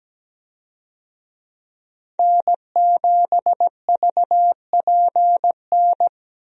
Morse code, heard as a keyed tone, about 17 wpm, 700 Hz. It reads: N7VPN
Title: USA Call Signs - 5 Characters - Repeated 2x - 17wpm